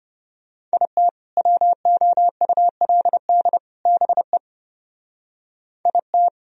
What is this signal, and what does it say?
Morse code, heard as a keyed tone, about 30 wpm, 700 Hz. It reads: IT WOULD BE IT
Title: Speed-Racing - Sets of 3 Words 25wpm